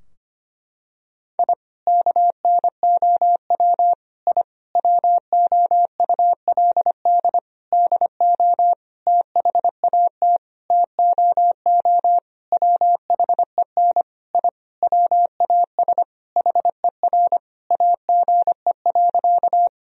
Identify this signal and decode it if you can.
Morse code, heard as a keyed tone, about 25 wpm, 700 Hz. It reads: I KNOW I WOULD DO THAT TOO WHEN I WAS HER AGE.
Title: Sentences from Top 1000 Words - Review Part 3 - 25wpm